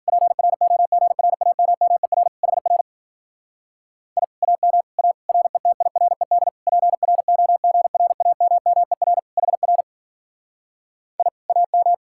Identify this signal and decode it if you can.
Morse code, heard as a keyed tone, about 48 wpm, 700 Hz. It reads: PROGRAMMER HR I AM A RETIRED PROGRAMMER HR I AM
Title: Speed-Racing Common QSO Phrases - 40wpm